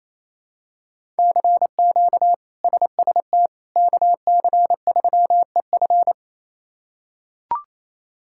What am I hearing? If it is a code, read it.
Morse code, heard as a keyed tone, about 28 wpm, 700 Hz. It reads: CQ SST KC3EF
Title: K1USN SST QSO Practice - 28wpm